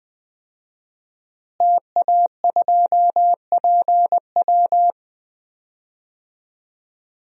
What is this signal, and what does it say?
Morse code, heard as a keyed tone, about 20 wpm, 700 Hz. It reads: TA2PW
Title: DXCC-100 Call Signs - 5 Characters - Repeated 2x - 20wpm